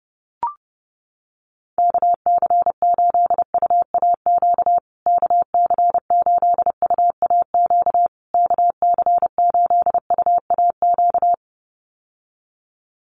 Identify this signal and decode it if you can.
Morse code, heard as a keyed tone, about 30 wpm, 700 Hz. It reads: KC8UAQ KC8UAQ KC8UAQ
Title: USA Call Signs Repeated 3x 30wpm